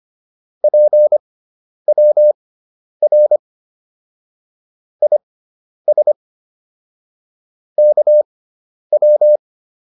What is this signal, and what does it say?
Morse code, heard as a keyed tone, about 25 wpm, spaced as for 10 wpm, 600 Hz. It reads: PWR IS KW
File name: QSO210 10